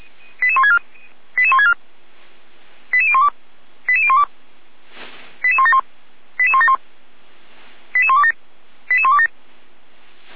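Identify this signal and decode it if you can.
Five-tone selective call (ZVEI), heard as quick runs of five tones, tones of about 70 ms. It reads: 88164 88164 88012 88012 88171 88171 88127 88127
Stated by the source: ZVEI five-tone selective call received on SDR, Feuerwehr Orlamünde fire brigade server.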